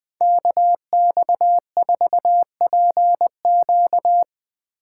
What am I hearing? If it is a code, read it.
Morse code, heard as a keyed tone, about 20 wpm, 700 Hz. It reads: KX4PQ
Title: K1USN SST QSO Practice - 20wpm